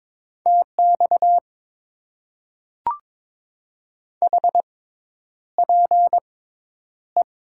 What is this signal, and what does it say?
Morse code, heard as a keyed tone, about 22 wpm, spaced as for 10 wpm, 700 Hz. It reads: TX HPE
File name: Top 90 QSO Elements - 6x Character Spacing - 22wpm